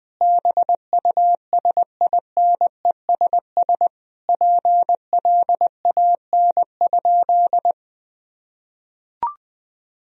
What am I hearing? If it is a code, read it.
Morse code, heard as a keyed tone, about 20 wpm, 700 Hz. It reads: BUSINESS PLAN?
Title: Sentences from Top 1000 Words - Review Part 3 - 20wpm